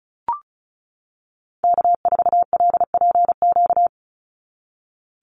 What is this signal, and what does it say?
Morse code, heard as a keyed tone, about 34 wpm, 700 Hz. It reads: K4LPQ
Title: NA SOTA Activator Callsigns 35wpm